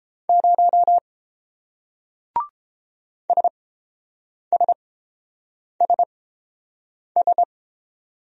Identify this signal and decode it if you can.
Morse code, heard as a keyed tone, about 29 wpm, spaced as for 7 wpm, 700 Hz. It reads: 0 SSSS